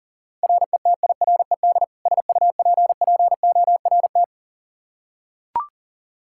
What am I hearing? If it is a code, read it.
Morse code, heard as a keyed tone, about 40 wpm, 690 Hz. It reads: RETIRED SUPPORT
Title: Practice QSO - Exchange 3 - More Info - 40wpm